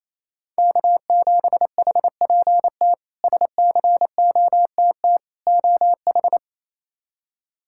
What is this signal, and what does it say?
Morse code, heard as a keyed tone, about 28 wpm, 700 Hz. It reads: K7HPT SCOTT OH
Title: K1USN SST QSO Practice - 28wpm